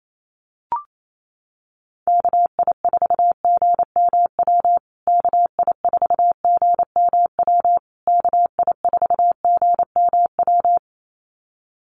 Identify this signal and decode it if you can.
Morse code, heard as a keyed tone, about 28 wpm, 700 Hz. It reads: KI4GMW KI4GMW KI4GMW
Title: USA Call Signs Repeated 3x 28wpm